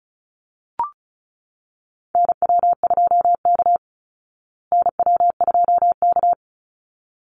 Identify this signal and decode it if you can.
Morse code, heard as a keyed tone, about 34 wpm, 700 Hz. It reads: NW2K NW2K
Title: Call Signs Repeated 3x - 35wpm